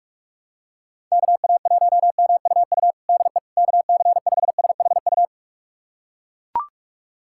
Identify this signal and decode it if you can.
Morse code, heard as a keyed tone, about 41 wpm, 700 Hz. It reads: KA1MUU DE KC5SHU
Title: USA Callsign DE Callsign - 6 Characters - 45wpm